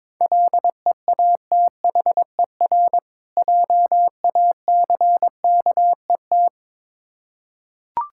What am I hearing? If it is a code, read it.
Morse code, heard as a keyed tone, about 22 wpm, 700 Hz. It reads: LEATHER JACKET
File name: Difficult Words with Hints - 22wpm